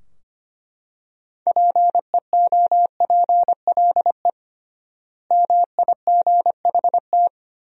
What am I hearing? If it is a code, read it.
Morse code, heard as a keyed tone, about 25 wpm, 700 Hz. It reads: PEOPLE MIGHT
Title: 3x Word-Spacing - Sentences from Top 500 Words - 25wpm